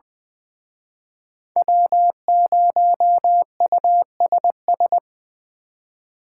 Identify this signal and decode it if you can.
Morse code, heard as a keyed tone, about 23 wpm, 700 Hz. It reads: W0USS
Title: Speed Racing - USA 1x3 Call Signs 20wpm